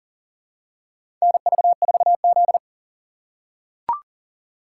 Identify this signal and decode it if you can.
Morse code, heard as a keyed tone, about 40 wpm, 700 Hz. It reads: NV4Z
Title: USA Call Signs - 6x Character Spacing - 40wpm